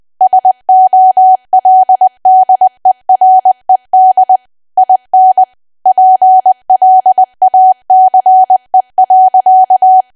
Morse code, heard as a keyed tone, about 20 wpm, 750 Hz. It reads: SOLDERED IN PLACE.
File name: cwa / W1AW-Level3 / 20WPM